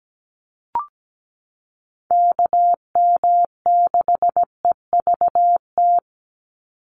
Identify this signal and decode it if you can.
Morse code, heard as a keyed tone, about 17 wpm, 700 Hz. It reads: KM6EVT
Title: USA Call Signs 17wpm